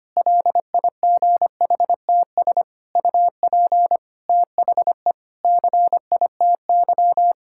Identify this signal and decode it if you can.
Morse code, heard as a keyed tone, about 25 wpm, 700 Hz. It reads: LIGHTS UP THE CITY
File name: practice / 25wpm / Difficult Sentences with Hints - 25wpm